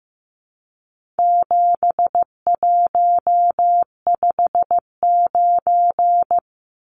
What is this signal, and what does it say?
Morse code, heard as a keyed tone, about 15 wpm, 700 Hz. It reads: 7159
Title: Numbers - 4-Digit 15wpm